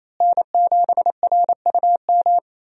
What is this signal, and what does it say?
Morse code, heard as a keyed tone, about 27 wpm, 700 Hz. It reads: N7RUM
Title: USA Call Signs - 5 Characters - Repeated 2x - 28wpm